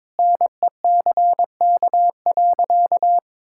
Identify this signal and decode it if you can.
Morse code, heard as a keyed tone, about 22 wpm, 700 Hz. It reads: NECK.
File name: practice / 22wpm / Sentences with 4-Word Phrases - 22wpm